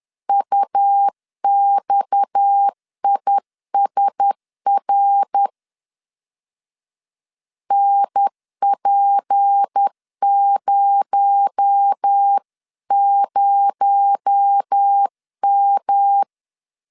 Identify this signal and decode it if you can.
Morse code, heard as a keyed tone, about 10 wpm, 800 Hz. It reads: UXISR NP00M
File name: CW / DET / 1 6 1500C S B 10wpm 4137